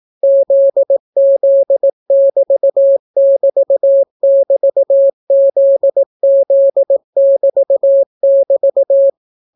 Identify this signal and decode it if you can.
Morse code, heard as a keyed tone, about 18 wpm, 550 Hz. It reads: ZZ<BT><BT><BT>ZZ<BT><BT>